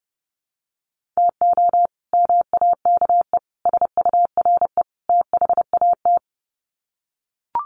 Morse code, heard as a keyed tone, about 30 wpm, 700 Hz. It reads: TO MAKE SURE THAT
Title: Sets of 4 Words 30wpm